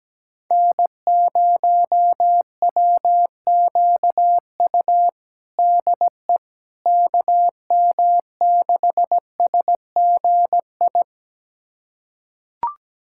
Morse code, heard as a keyed tone, about 17 wpm, 700 Hz. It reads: N0WQU DE KM6SGI
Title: USA Callsign DE Callsign 17wpm